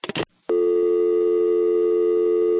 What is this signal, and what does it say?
Dial tone (North America), 350 + 440 Hz.
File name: phone numbers / calls / toluca security var1a